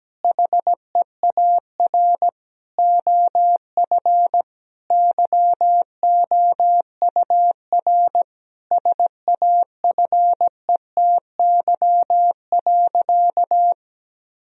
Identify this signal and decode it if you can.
Morse code, heard as a keyed tone, about 17 wpm, 700 Hz. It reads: HEAR OF YOUR SAFETY.
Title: Sentences from Top 1000 Words - Review Part 4 - 17wpm